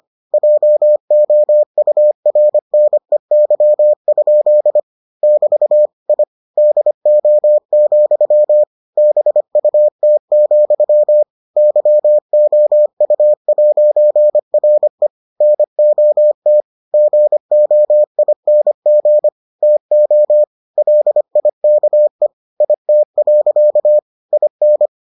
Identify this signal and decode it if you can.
Morse code, heard as a keyed tone, about 25 wpm, 600 Hz. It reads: JOURNEY? <BT> I DO, BUT, YOU'RE NOT GOING TO LIKE IT. IN